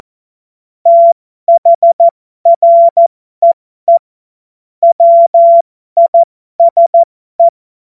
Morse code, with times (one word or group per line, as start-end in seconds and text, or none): 0.9-4.0 THREE
4.8-7.5 WISE